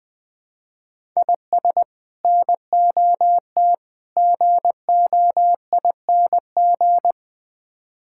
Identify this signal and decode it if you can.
Morse code, heard as a keyed tone, about 20 wpm, 700 Hz. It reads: IS NOT GOING
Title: Build A Phrase - 4-Word Phrases - 20wpm